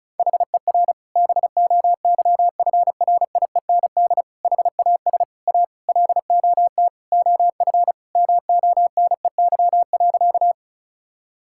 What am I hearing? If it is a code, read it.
Morse code, heard as a keyed tone, about 34 wpm, 700 Hz. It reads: HER BOYFRIEND HAS A LOT OF MONEY.
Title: Sentences from Top 1000 Words - Review Part 5 - 35wpm